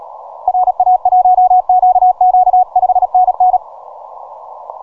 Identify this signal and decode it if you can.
Morse code, heard as a keyed tone, about 37 wpm, 730 Hz. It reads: RA1QQ 5NN
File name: simple-file-list / VP8STI 12MCW